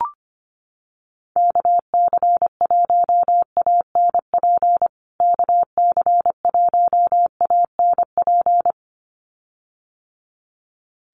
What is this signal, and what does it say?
Morse code, heard as a keyed tone, about 25 wpm, 700 Hz. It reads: KC1ANP KC1ANP
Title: USA Call Signs Repeated 2x 25wpm